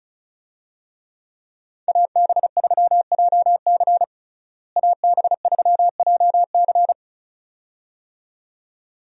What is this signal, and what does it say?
Morse code, heard as a keyed tone, about 36 wpm, 700 Hz. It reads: AB3JC AB3JC
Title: USA Call Signs - 5 Characters - Repeated 2x - 35wpm